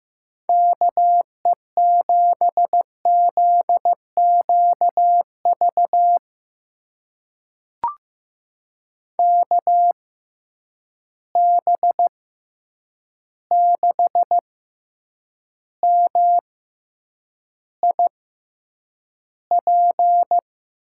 Morse code, heard as a keyed tone, about 15 wpm, spaced as for 8 wpm, 700 Hz. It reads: KE7ZQV KB6MIP
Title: USA Call Signs - 6 Characters - 6x Character Spacing - 15wpm